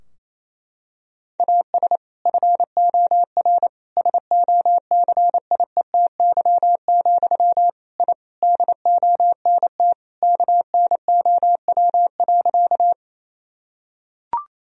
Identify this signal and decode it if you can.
Morse code, heard as a keyed tone, about 28 wpm, 700 Hz. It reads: AS FOR SOCIETY, I DONT KNOW.
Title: Sentences from Top 1000 Words - Review Part 4 - 28wpm